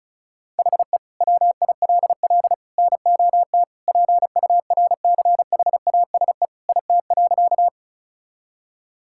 Morse code, heard as a keyed tone, about 36 wpm, 700 Hz. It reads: HE WILL NOT PURCHASE IT.
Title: Sentences from Top 1000 Words - Review Part 5 - 35wpm